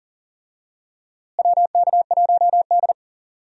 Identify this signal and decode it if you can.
Morse code, heard as a keyed tone, about 40 wpm, 700 Hz. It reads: WK1D
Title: USA Call Signs - 4 Characters - Repeated 2x - 40wpm